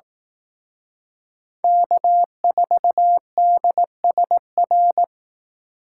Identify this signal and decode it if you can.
Morse code, heard as a keyed tone, about 18 wpm, 700 Hz. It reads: K4DSR
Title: Speed Racing - USA 1x3 Call Signs 15wpm